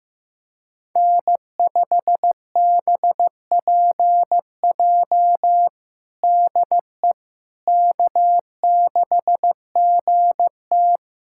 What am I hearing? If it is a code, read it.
Morse code, heard as a keyed tone, about 15 wpm, 700 Hz. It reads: N5BPJ DE K6GT
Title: K1USN SST QSO Practice 15wpm